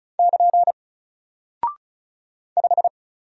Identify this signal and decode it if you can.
Morse code, heard as a keyed tone, about 36 wpm, 700 Hz. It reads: <KN> 5
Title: Single Character (Letter:Number:Punctuation:Prosign) 35wpm